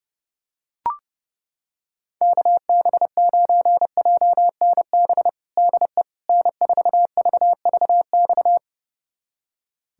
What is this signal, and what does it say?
Morse code, heard as a keyed tone, about 30 wpm, 700 Hz. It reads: KB9JNB DE N4VVX
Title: USA Callsign DE Callsign 30wpm